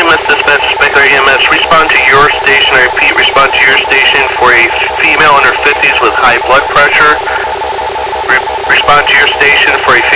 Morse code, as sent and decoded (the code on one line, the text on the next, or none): none